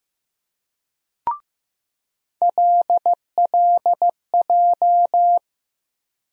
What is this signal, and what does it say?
Morse code, heard as a keyed tone, about 15 wpm, 700 Hz. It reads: LLJ
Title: USA Call Sign Postfixes - 3 Letters 15wpm